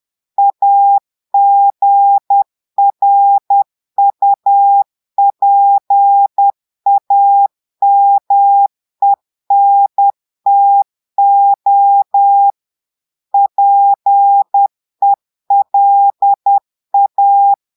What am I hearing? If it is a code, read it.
Morse code, heard as a keyed tone, about 10 wpm, 800 Hz. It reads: AGRUPAMENTO PELA
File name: Portugais - texte en clair 1 - vitesse - 10 mots minute